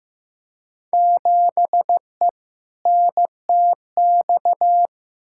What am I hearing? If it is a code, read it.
Morse code, heard as a keyed tone, about 15 wpm, 700 Hz. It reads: 7E NTX
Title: Field Day Exchanges Encore - 15wpm